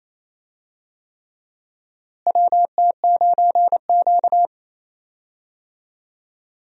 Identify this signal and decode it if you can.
Morse code, heard as a keyed tone, about 28 wpm, 700 Hz. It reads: WT9Q
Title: USA Call Signs - 4 Characters - Repeated 3x - 28wpm